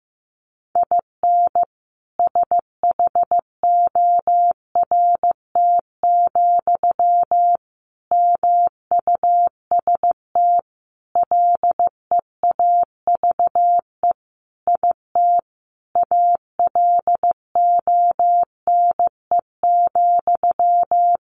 Morse code, heard as a keyed tone, about 15 wpm, 700 Hz. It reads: IN SHORT, MUST LEAVE IT ALONE,
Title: Sentences from Top 1000 Words 15wpm